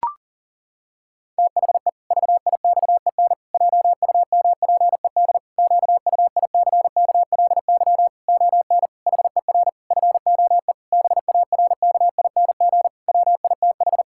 Morse code, heard as a keyed tone, about 40 wpm, 700 Hz. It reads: THE VIXEN JUMPED QUICKLY ON HER FOE BARKING WITH
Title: Pangram Challenge v2 - 40wpm